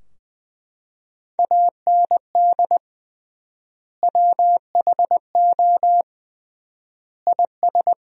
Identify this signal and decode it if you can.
Morse code, heard as a keyed tone, about 20 wpm, 700 Hz. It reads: AND WHO IS
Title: 3x Word-Spacing - Sentences from Top 300 Words - 20wpm